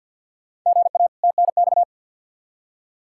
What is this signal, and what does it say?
Morse code, heard as a keyed tone, about 52 wpm, 700 Hz. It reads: GA TNX
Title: K1USN SST QSO Practice 50wpm